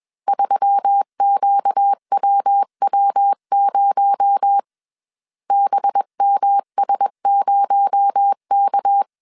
Morse code, heard as a keyed tone, about 21 wpm, 800 Hz. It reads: 3QWW0 BMS0K
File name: CW / DET / 1 6 1500C S B 20wpm 2048